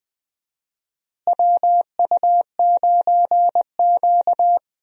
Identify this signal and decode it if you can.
Morse code, heard as a keyed tone, about 20 wpm, 700 Hz. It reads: WU9Q
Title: USA Call Signs - 4 Characters - Repeated 2x - 20wpm